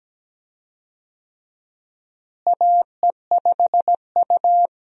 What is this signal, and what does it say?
Morse code, heard as a keyed tone, about 17 wpm, 700 Hz. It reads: AE5U
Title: USA Call Signs - 4 Characters - Repeated 2x - 17wpm